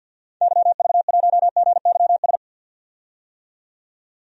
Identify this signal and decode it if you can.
Morse code, heard as a keyed tone, about 49 wpm, 700 Hz. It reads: YV1GYS
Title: DXCC-100 Call Signs - 6 Characters - 50wpm